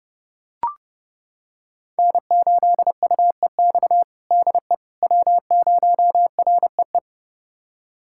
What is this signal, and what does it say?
Morse code, heard as a keyed tone, about 30 wpm, 700 Hz. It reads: N8UEX DE W0REE
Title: USA Callsign DE Callsign - 5 Characters - 30wpm